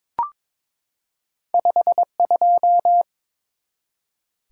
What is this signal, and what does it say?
Morse code, heard as a keyed tone, about 22 wpm, 700 Hz. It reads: 52
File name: practice / 22wpm / Two Letter-Number Combinations - 22wpm